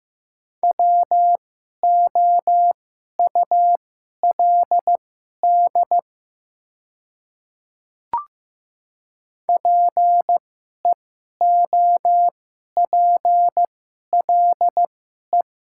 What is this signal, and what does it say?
Morse code, heard as a keyed tone, about 15 wpm, spaced as for 11 wpm, 700 Hz. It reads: WOULD PEOPLE
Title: 2x Character Spacing - Top 200 Words - 15wpm